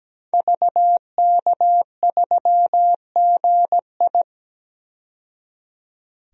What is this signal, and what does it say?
Morse code, heard as a keyed tone, about 17 wpm, 700 Hz. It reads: VK3GI
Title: DXCC-100 Call Signs - 5 Characters - 17wpm